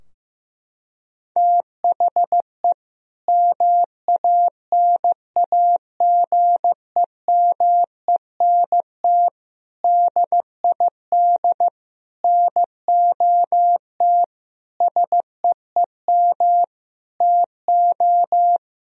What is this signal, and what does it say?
Morse code, heard as a keyed tone, about 16 wpm, 700 Hz. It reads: THE MANAGEMENT DID NOT SEEM TO
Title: Sentences from Top 1000 Words - Review Part 4 - 15wpm